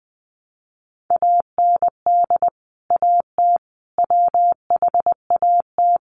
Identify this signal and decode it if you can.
Morse code, heard as a keyed tone, about 20 wpm, 700 Hz. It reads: AND AT WHAT
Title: Sentences from Top 1200 Words 20wpm